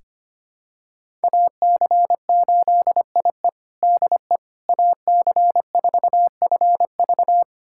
Morse code, heard as a keyed tone, about 25 wpm, 700 Hz. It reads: AC8IE DE AC4FV
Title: USA Callsign DE Callsign - 5 Characters - 25wpm